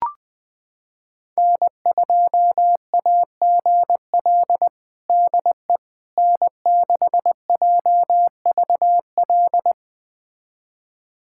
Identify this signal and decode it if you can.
Morse code, heard as a keyed tone, about 20 wpm, 700 Hz. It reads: N2AGL DE N6JVL
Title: USA Callsign DE Callsign - 5 Characters - 20wpm